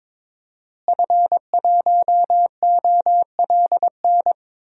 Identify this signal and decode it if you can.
Morse code, heard as a keyed tone, about 22 wpm, 700 Hz. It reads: F1OLN
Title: DXCC-100 Call Signs - 5 Characters - Repeated 3x - 22wpm